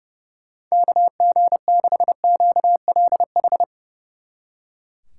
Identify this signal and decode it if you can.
Morse code, heard as a keyed tone, about 33 wpm, 700 Hz. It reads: KG6QLH
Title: Speed Racing - USA Call Signs 30wpm